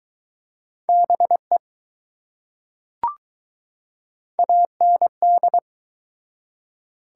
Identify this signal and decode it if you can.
Morse code, heard as a keyed tone, about 20 wpm, 700 Hz. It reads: BE AND